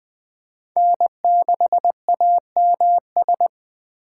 Morse code, heard as a keyed tone, about 20 wpm, 700 Hz. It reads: N6AMS